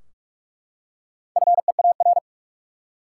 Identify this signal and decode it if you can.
Morse code, heard as a keyed tone, about 45 wpm, 700 Hz. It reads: FEAR
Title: Four Letter Words - 45wpm